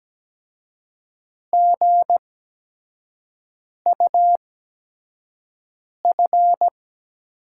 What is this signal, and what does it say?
Morse code, heard as a keyed tone, about 17 wpm, spaced as for 5 wpm, 700 Hz. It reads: GUF
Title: USA Call Signs - 6 Characters - 8x Character Spacing - 17wpm